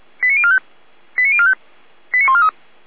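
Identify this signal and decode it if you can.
Five-tone selective call (ZVEI), heard as quick runs of five tones, tones of about 70 ms. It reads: 89045 89045 89143
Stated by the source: ZVEI five-tone selective call received on an SDR.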